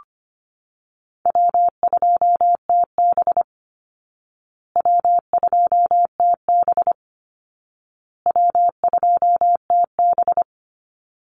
Morse code, heard as a keyed tone, about 25 wpm, 700 Hz. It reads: W2TB W2TB W2TB
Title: Easy Call Signs Repeated 3x - 25wpm